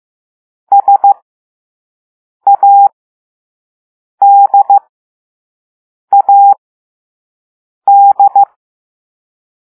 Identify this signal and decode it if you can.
Morse code, heard as a keyed tone, about 15 wpm, spaced as for 5 wpm, 800 Hz. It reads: SADAD
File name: Morse / Level3B